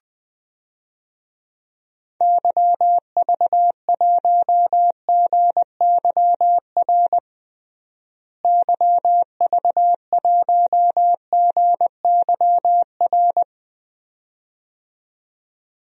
Morse code, heard as a keyed tone, about 20 wpm, 700 Hz. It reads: YV1GYR YV1GYR
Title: DXCC-100 Call Signs - 6 Characters - Repeated 2x - 20wpm